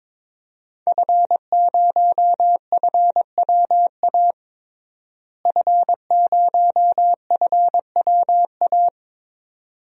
Morse code, heard as a keyed tone, about 21 wpm, 700 Hz. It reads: F0FWA F0FWA
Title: DXCC-100 Call Signs - 5 Characters - Repeated 2x - 22wpm